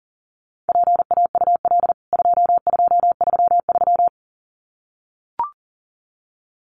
Morse code, heard as a keyed tone, about 37 wpm, 700 Hz. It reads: PAUL 2233
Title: CWT Exchange 40wpm